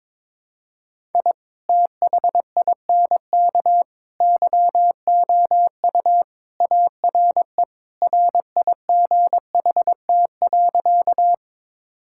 Morse code, heard as a keyed tone, about 22 wpm, 700 Hz. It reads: I THINK YOU ARE RIGHT.
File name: practice / 22wpm / Sentences with 3-Word Phrases - 22wpm